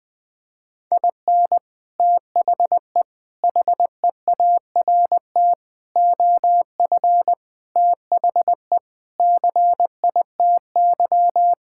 Morse code, heard as a keyed tone, about 19 wpm, 700 Hz. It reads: IN THE HEART OF THE CITY
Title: Sentences with 5-Word Phrases - Spoken Beginning - 20wpm